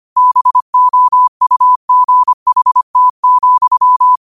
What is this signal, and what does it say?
Morse code, heard as a keyed tone, about 25 wpm, 1000 Hz. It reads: BOUGHT,
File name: qotdfiles / 25wpm